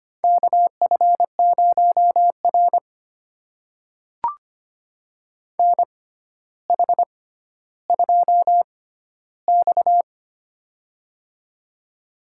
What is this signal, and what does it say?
Morse code, heard as a keyed tone, about 25 wpm, spaced as for 13 wpm, 700 Hz. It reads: KF0R NH2X
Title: USA Call Signs - 4 Characters - 6x Character Spacing - 25wpm